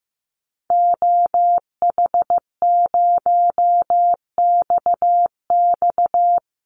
Morse code, heard as a keyed tone, about 15 wpm, 700 Hz. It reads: OH0XX
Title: Call Signs Repeated 3x - 15wpm